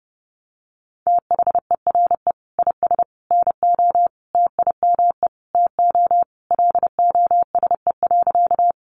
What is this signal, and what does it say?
Morse code, heard as a keyed tone, about 28 wpm, 700 Hz. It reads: THERE IS NO TIME TO LOSE.
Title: Sentences from Top 1100 Words 30wpm